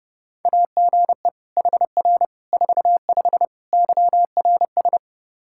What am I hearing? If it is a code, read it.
Morse code, heard as a keyed tone, about 32 wpm, 700 Hz. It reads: AGE HR 45 YRS
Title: K1USN SST QSO Practice 30wpm